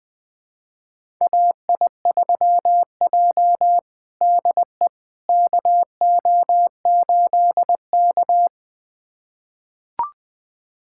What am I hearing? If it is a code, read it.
Morse code, heard as a keyed tone, about 20 wpm, 700 Hz. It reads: AI3J DE KO8K
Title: USA Callsign DE Callsign - 4 Characters - 20wpm